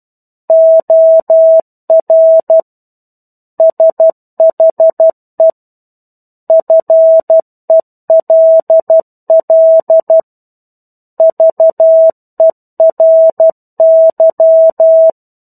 Morse code, heard as a keyed tone, about 12 wpm, 650 Hz. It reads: OR SHE FELL VERY
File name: 01-Chapter-01 12 WPM